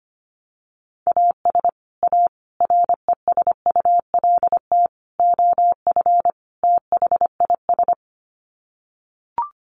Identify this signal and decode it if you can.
Morse code, heard as a keyed tone, about 25 wpm, 700 Hz. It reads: AS A RESULT OF THIS
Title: Sets of 5 Words 25wpm